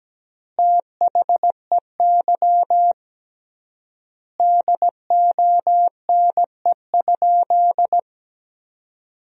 Morse code, heard as a keyed tone, about 17 wpm, 700 Hz. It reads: THEY DONE?
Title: 3x Word-Spacing - Sentences from Top 500 Words - 17wpm